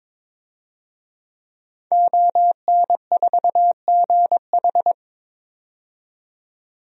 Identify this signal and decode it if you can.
Morse code, heard as a keyed tone, about 22 wpm, 700 Hz. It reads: ON4GH
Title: DXCC-100 Call Signs - 5 Characters - 22wpm